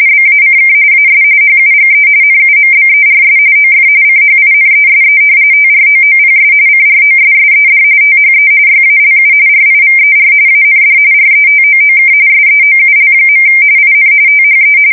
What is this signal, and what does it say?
RTTY, about 45.45 baud. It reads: BACK AS HUGE NUMBERS OF CHINESE TROOPS CROSSED THE YALU RIVER AND RETOOK SEOUL. IN THE
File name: fsk170 / battleshipnj